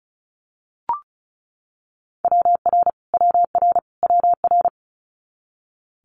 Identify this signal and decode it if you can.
Morse code, heard as a keyed tone, about 36 wpm, 700 Hz. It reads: WR WR WR
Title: Multi-letter Phonemes Repeated 3x 35wpm